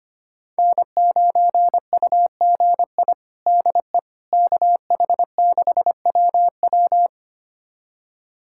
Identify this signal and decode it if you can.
Morse code, heard as a keyed tone, about 25 wpm, 700 Hz. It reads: N9UGI DE KH6WW
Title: USA Callsign DE Callsign - 5 Characters - 25wpm